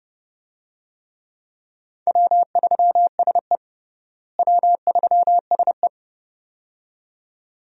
Morse code, heard as a keyed tone, about 32 wpm, 700 Hz. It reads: W3SE W3SE
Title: USA Call Signs - 4 Characters - Repeated 2x - 30wpm